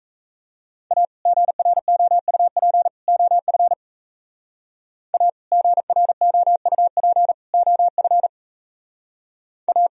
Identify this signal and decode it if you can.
Morse code, heard as a keyed tone, about 37 wpm, 700 Hz. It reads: A GROUP OF A GROUP OF A